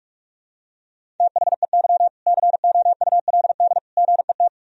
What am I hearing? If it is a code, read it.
Morse code, heard as a keyed tone, about 46 wpm, 700 Hz. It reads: THEY COULD GET
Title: Sentences from Top 1000 Words - Review Part 3 - 45wpm